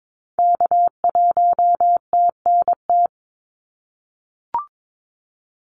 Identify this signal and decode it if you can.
Morse code, heard as a keyed tone, about 22 wpm, 700 Hz. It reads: K1TNT
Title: NA SOTA Activator Callsigns 22wpm